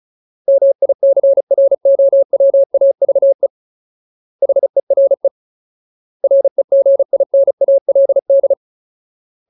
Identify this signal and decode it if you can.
Morse code, heard as a keyed tone, about 34 wpm, 550 Hz. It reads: MICROWAVE HERE REGINALD
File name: Ultimate Word List / 35WPM Ultimate